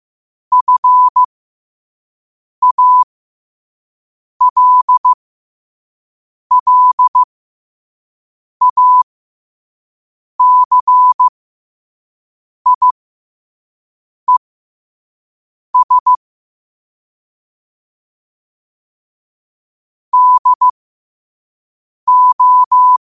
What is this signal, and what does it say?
Morse code, heard as a keyed tone, about 15 wpm, spaced as for 5 wpm, 1000 Hz. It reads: FALLACIES DO
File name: qotdfiles / 5wpm